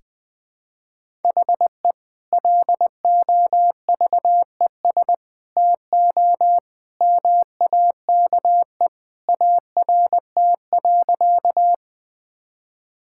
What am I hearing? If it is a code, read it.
Morse code, heard as a keyed tone, about 21 wpm, 700 Hz. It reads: HE LOVES TO MAKE ART.
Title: Sentences with 2-Word Phrases - 20wpm